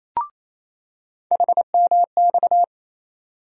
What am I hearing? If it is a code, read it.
Morse code, heard as a keyed tone, about 28 wpm, 700 Hz. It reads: HMX
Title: USA Call Sign Postfixes - 3 Letters 28wpm